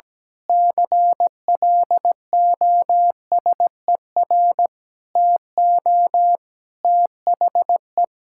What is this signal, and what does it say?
Morse code, heard as a keyed tone, about 17 wpm, 700 Hz. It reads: CLOSER TO THE
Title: Pangram Challenge v2 - 17wpm